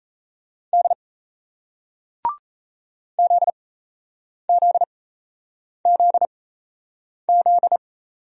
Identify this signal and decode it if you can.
Morse code, heard as a keyed tone, about 38 wpm, spaced as for 8 wpm, 700 Hz. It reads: D ZZZZ